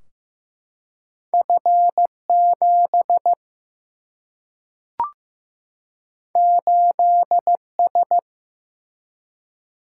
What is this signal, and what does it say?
Morse code, heard as a keyed tone, about 15 wpm, 700 Hz. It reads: F 7 8 S
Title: Single-Letter Single-Number 15wpm